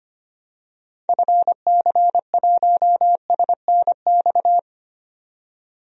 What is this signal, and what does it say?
Morse code, heard as a keyed tone, about 25 wpm, 700 Hz. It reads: FC1SNX
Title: DXCC-100 Call Signs - 6 Characters - 25wpm